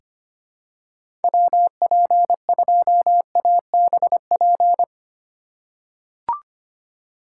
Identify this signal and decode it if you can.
Morse code, heard as a keyed tone, about 25 wpm, 700 Hz. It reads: WP2ABP
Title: USA Call Signs - 6 Characters - 3x Character Spacing - 25wpm